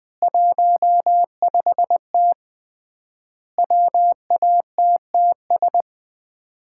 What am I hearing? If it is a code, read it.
Morse code, heard as a keyed tone, about 20 wpm, 700 Hz. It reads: 15T WATTS
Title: Common QSO Phrases - 3x Word-Spacing - 20wpm